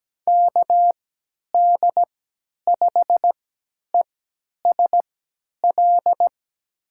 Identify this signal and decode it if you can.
Morse code, heard as a keyed tone, about 17 wpm, spaced as for 10 wpm, 700 Hz. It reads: KD5ESL
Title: USA Call Signs - 3x Character Spacing - 17wpm